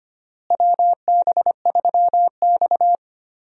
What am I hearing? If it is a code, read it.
Morse code, heard as a keyed tone, about 25 wpm, 700 Hz. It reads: WB3X
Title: USA Call Signs - 4 Characters - Repeated 3x - 25wpm